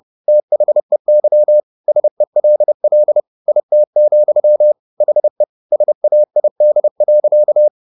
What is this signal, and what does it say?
Morse code, heard as a keyed tone, about 30 wpm, 600 Hz. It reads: THEY SELL IT, HE SAID.